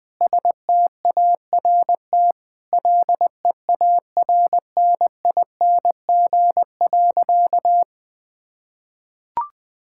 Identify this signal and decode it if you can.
Morse code, heard as a keyed tone, about 20 wpm, 700 Hz. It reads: START LEARNING.
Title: Sentences with 7-Word Phrases - Spoken Beginning - 20wpm